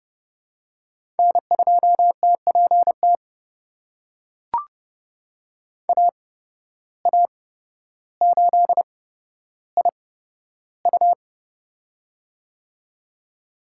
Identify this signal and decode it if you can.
Morse code, heard as a keyed tone, about 30 wpm, spaced as for 13 wpm, 700 Hz. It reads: N2TPT AA8IU
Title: USA Call Signs - 5 Characters - 8x Character Spacing - 30wpm